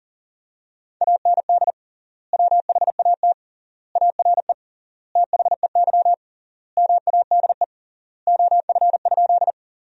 Morse code, heard as a keyed tone, about 40 wpm, 700 Hz. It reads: AND WHAT ARE THEY MADE OF?
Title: 3x Word-Spacing - Sentences from Top 200 Words - 40wpm